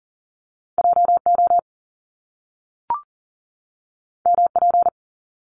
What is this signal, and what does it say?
Morse code, heard as a keyed tone, about 39 wpm, 700 Hz. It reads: JO MP
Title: Top Two Letter Combinations in Words 40wpm